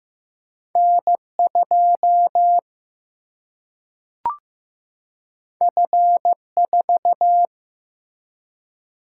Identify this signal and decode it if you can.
Morse code, heard as a keyed tone, about 15 wpm, 700 Hz. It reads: N 2 F 4
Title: Single-Letter Single-Number 15wpm